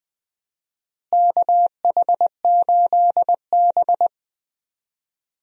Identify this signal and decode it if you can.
Morse code, heard as a keyed tone, about 20 wpm, 700 Hz. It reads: KH8B
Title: USA Call Signs - 4 Characters - Repeated 2x - 20wpm